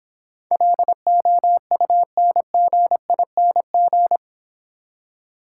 Morse code, heard as a keyed tone, about 25 wpm, 700 Hz. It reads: LOUNGING